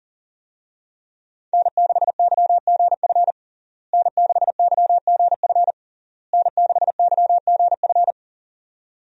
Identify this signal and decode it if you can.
Morse code, heard as a keyed tone, about 40 wpm, 700 Hz. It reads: N6YGF N6YGF N6YGF
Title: USA Call Signs - 5 Characters - Repeated 3x - 40wpm